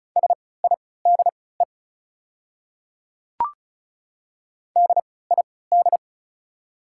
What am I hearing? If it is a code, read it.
Morse code, heard as a keyed tone, about 33 wpm, spaced as for 19 wpm, 690 Hz. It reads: SIDE DID
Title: 3x Character Spacing - Top 100 Words - 35wpm